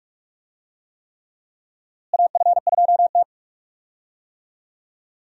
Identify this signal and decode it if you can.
Morse code, heard as a keyed tone, about 42 wpm, 700 Hz. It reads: AF2T
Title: USA Call Signs - 4 Characters - Repeated 3x - 45wpm